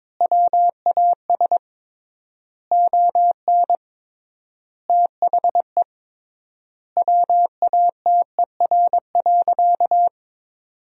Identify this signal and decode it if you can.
Morse code, heard as a keyed tone, about 22 wpm, 700 Hz. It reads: WAS ON THE WATER.
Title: 3x Word-Spacing - Sentences from Top 500 Words - 22wpm